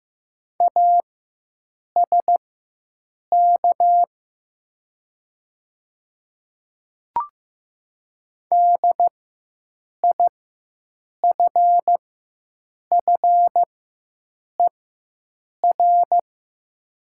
Morse code, heard as a keyed tone, about 15 wpm, spaced as for 7 wpm, 690 Hz. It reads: ASK DIFFER
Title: 4x Character Spacing - Top 200 Words - 15wpm